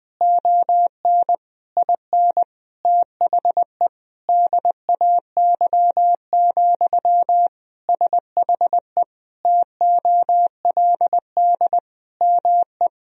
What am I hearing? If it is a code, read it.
Morse code, heard as a keyed tone, about 20 wpm, 700 Hz. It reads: ON IN THE DAY, SHE TOLD ME
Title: Sentences from Top 1000 Words - Review Part 4 - 20wpm